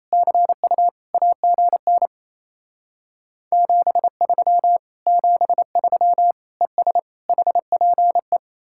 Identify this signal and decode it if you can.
Morse code, heard as a keyed tone, about 37 wpm, 700 Hz. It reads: CU AGN 73 73 ES HPE